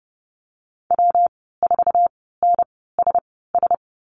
Morse code, heard as a keyed tone, about 30 wpm, spaced as for 17 wpm, 700 Hz. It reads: W4NSS
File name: Lesson 19g - Letter w - Call Signs 30wpm